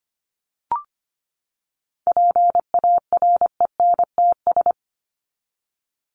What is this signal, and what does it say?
Morse code, heard as a keyed tone, about 25 wpm, 700 Hz. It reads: PARENTS
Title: ES:S Suffix - Top 100 Words with Base Words 25wpm